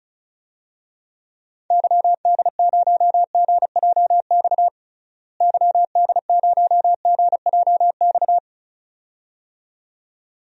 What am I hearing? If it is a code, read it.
Morse code, heard as a keyed tone, about 34 wpm, 700 Hz. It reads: YD0GJX YD0GJX
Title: DXCC-100 Call Signs - 6 Characters - Repeated 2x - 35wpm